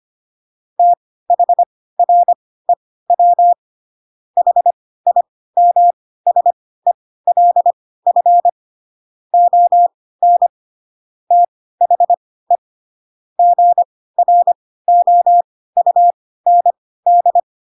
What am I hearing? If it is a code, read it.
Morse code, heard as a keyed tone, about 25 wpm, spaced as for 16 wpm, 700 Hz. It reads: THREW HIMSELF ON THE GROUND
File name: Pinocchio / Pinocchio ch3 10 25 ik1hge